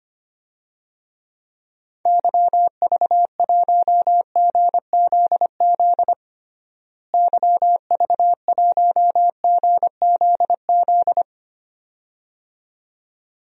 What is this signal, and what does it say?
Morse code, heard as a keyed tone, about 25 wpm, 700 Hz. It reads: YV1GZZ YV1GZZ
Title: DXCC-100 Call Signs - 6 Characters - Repeated 2x - 25wpm